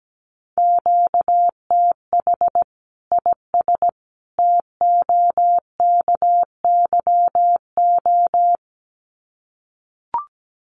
Morse code, heard as a keyed tone, about 17 wpm, 700 Hz. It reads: QTH IS TOKYO
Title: Common QSO Phrases 17wpm